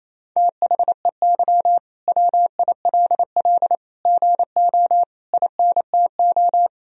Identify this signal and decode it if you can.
Morse code, heard as a keyed tone, about 27 wpm, 700 Hz. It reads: THEY WILL GO INTO
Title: Sentences from Top 1000 Words - Review Part 4 - 28wpm